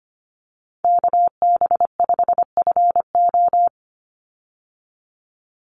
Morse code, heard as a keyed tone, about 25 wpm, 700 Hz. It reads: KB5FO
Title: USA Call Signs 25wpm